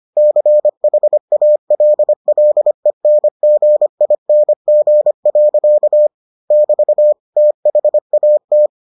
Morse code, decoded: CHALLENGING. <BT> THAT

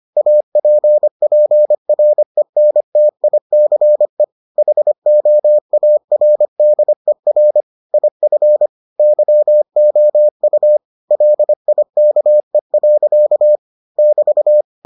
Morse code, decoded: APPRENTICE HOARDER IF YOU LIKE. <BT>

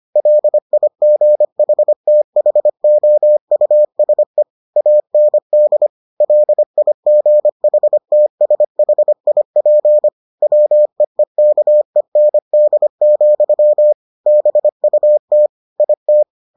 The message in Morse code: LIGHTHOUSE AND LIGHTSHIP WEEKEND, BUT IT

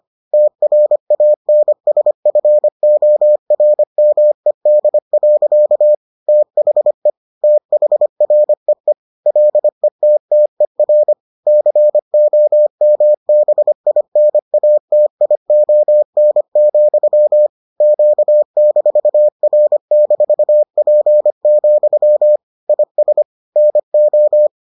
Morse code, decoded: TRANSFORMED. THE THREE LETTER COMBINATION, Q-R-P, IS NO

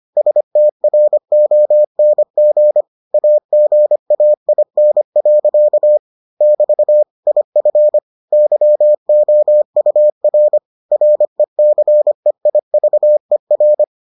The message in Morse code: STRONG AGAIN. <BT> IF YOUR RECEIVER

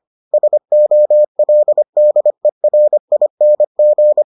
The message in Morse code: SOLDERING

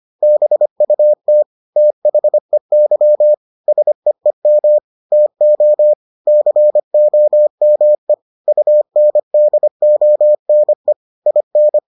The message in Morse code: BUT THEY SEEM TO COME UNDONE IN

